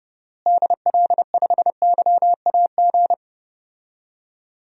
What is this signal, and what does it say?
Morse code, heard as a keyed tone, about 31 wpm, 700 Hz. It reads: DL5YAG
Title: DXCC-100 Call Signs - 6 Characters - Repeated 3x - 30wpm